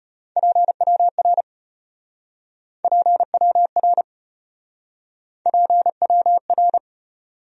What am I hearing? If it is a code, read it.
Morse code, heard as a keyed tone, about 33 wpm, 700 Hz. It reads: PWR PWR PWR